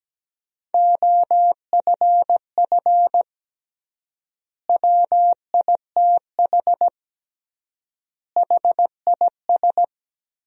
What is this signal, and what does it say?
Morse code, heard as a keyed tone, about 17 wpm, 700 Hz. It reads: OFF WITH HIS
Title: 3x Word-Spacing - Sentences from Top 300 Words - 17wpm